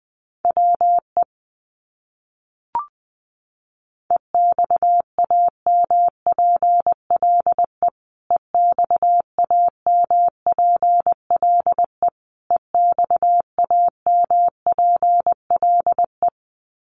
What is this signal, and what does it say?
Morse code, heard as a keyed tone, about 20 wpm, 700 Hz. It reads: WE EXAMPLE EXAMPLE EXAMPLE
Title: Top 300 Words Repeated 20wpm